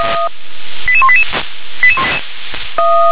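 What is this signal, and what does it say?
Five-tone selective call (ZVEI), heard as quick runs of five tones, tones of about 70 ms. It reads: 88188
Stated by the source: ZVEI five-tone selective call received by SDR.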